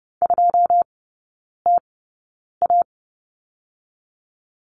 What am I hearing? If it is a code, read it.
Morse code, heard as a keyed tone, about 30 wpm, spaced as for 9 wpm, 700 Hz. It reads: 2TA
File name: 081 Lesson 14e - Number 2 - Three Characters 30wpm